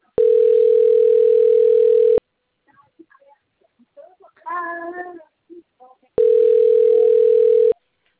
Ringback tone (North America), 440 + 480 Hz.
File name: out- - - - - .38023